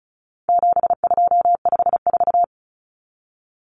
Morse code, heard as a keyed tone, about 34 wpm, 700 Hz. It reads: 7254
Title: Common Cut Numbers - 4-Digit 35wpm